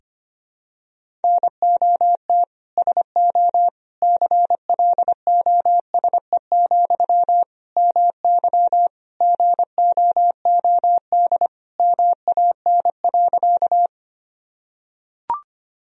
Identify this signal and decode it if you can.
Morse code, heard as a keyed tone, about 25 wpm, 700 Hz. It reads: NOT SO CLOSE, MY GOOD MAN.
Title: Sentences from Top 1000 Words - Review Part 1 - 25wpm